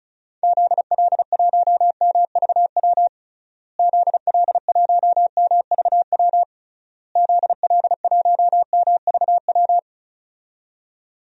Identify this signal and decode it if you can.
Morse code, heard as a keyed tone, about 34 wpm, 700 Hz. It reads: ZL1MVW ZL1MVW ZL1MVW
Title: DXCC-100 Call Signs - 6 Characters - Repeated 3x - 35wpm